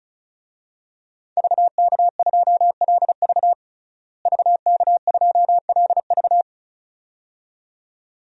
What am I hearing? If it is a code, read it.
Morse code, heard as a keyed tone, about 35 wpm, 700 Hz. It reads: VK2LV VK2LV
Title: DXCC-100 Call Signs - 5 Characters - Repeated 2x - 35wpm